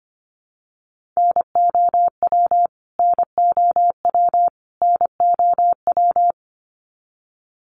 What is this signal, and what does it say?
Morse code, heard as a keyed tone, about 25 wpm, 700 Hz. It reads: NOW NOW NOW
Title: Top 300 Words Repeated 25wpm